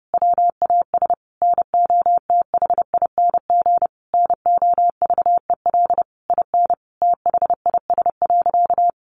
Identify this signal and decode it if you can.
Morse code, heard as a keyed tone, about 30 wpm, 700 Hz. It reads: WAS NOTHING NOVEL IN THIS.
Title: Sentences from Top 1300 Words 30wpm